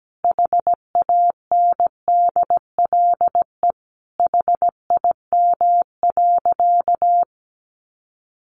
Morse code, heard as a keyed tone, about 17 wpm, 700 Hz. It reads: HANDLE HIM.
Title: Sentences from Top 1100 Words 17wpm